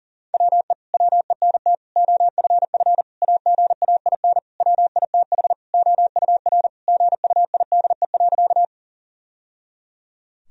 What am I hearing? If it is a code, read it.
Morse code, heard as a keyed tone, about 40 wpm, 700 Hz. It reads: WE WENT OFF AGAIN WITH OUR GUIDE.
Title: Sentences from Top 1000 Words - Review Part 3 - 40wpm